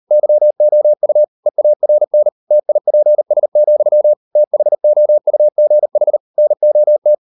Morse code, heard as a keyed tone, about 38 wpm, 600 Hz. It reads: YOU EARN TIPS, THOUGH NOT